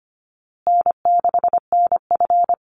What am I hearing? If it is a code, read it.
Morse code, heard as a keyed tone, about 25 wpm, 700 Hz. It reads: N6NF
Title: Call Signs Repeated 2x - 25wpm